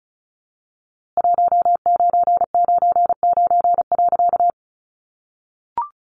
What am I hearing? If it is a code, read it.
Morse code, heard as a keyed tone, about 34 wpm, 700 Hz. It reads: 1999.
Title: A Sound Of Thunder 35wpm